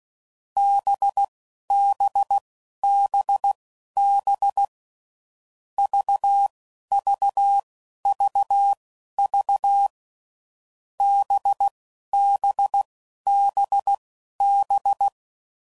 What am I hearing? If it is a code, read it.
Morse code, heard as a keyed tone, about 16 wpm, spaced as for 11 wpm, 800 Hz. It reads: BBBB VVVV BBBB
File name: Groups / Basic Letter Groups / VB46